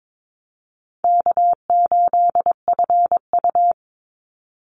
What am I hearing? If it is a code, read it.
Morse code, heard as a keyed tone, about 22 wpm, 700 Hz. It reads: K8FU
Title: Easy Call Signs Repeated 3x - 22wpm